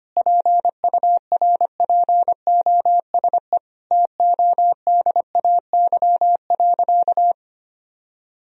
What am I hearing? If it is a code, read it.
Morse code, heard as a keyed tone, about 25 wpm, 700 Hz. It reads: PURPOSE TODAY.
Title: Sentences from Top 1000 Words - Review Part 4 - 25wpm